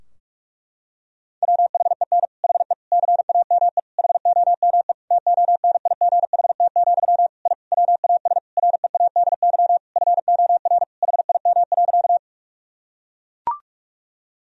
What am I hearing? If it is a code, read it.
Morse code, heard as a keyed tone, about 41 wpm, 700 Hz. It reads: WHEN HE CAME HOME TONIGHT, I WAS READY FOR HIM.
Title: Sentences from Top 1000 Words - Review Part 3 - 45wpm